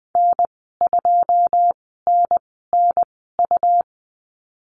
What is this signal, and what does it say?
Morse code, heard as a keyed tone, about 20 wpm, spaced as for 15 wpm, 700 Hz. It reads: N2NNU
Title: 147 Lesson 24g - Letter y - Call Signs 20wpm